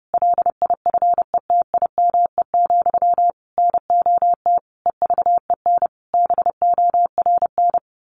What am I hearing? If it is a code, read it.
Morse code, heard as a keyed tone, about 30 wpm, 700 Hz. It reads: LIFETIME, NOT EVEN BORN
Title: A Sound Of Thunder 30wpm